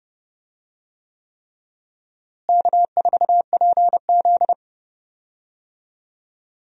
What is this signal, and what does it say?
Morse code, heard as a keyed tone, about 30 wpm, 700 Hz. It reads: K4PZ
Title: USA Call Signs - 4 Characters - Repeated 3x - 30wpm